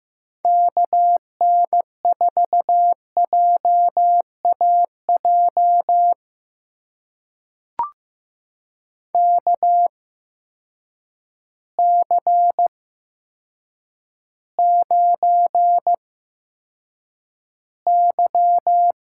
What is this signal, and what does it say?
Morse code, heard as a keyed tone, about 15 wpm, spaced as for 8 wpm, 700 Hz. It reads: KN4JAJ KC9Y
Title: USA Call Signs - 6 Characters - 8x Character Spacing - 15wpm